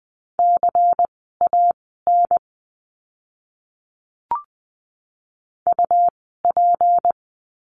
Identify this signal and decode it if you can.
Morse code, heard as a keyed tone, about 20 wpm, spaced as for 15 wpm, 700 Hz. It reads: CAN UP
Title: 199 Lesson 32f - Letter k - Words 20wpm